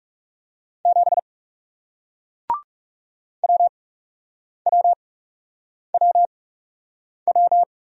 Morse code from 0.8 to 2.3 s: 7

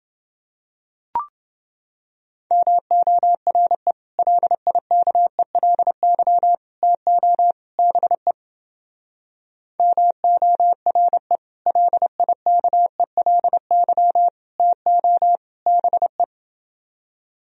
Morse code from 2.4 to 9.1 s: MORE LIKELY TO BE

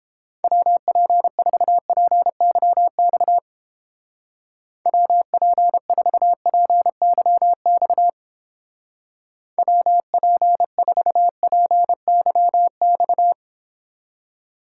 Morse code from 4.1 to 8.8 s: WP4PYX

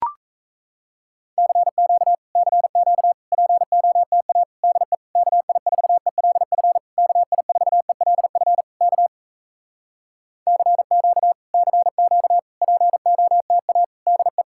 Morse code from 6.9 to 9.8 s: KI4ELF K

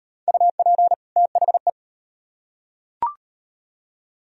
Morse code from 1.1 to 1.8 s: THE

